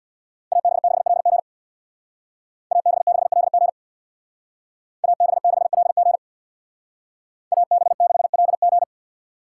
Morse code, heard as a keyed tone, about 64 wpm, 700 Hz. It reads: AB6LG AB6LG AB6LG AB6LG